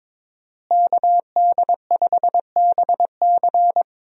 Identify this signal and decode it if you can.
Morse code, heard as a keyed tone, about 22 wpm, 700 Hz. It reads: KD5BC
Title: USA Call Signs - 5 Characters - Repeated 2x - 22wpm